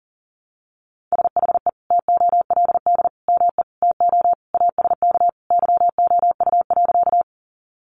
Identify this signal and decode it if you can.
Morse code, heard as a keyed tone, about 40 wpm, 700 Hz. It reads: SHE TOLD ME TO ASK YOU.
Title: Sentences from Top 400 Words 40wpm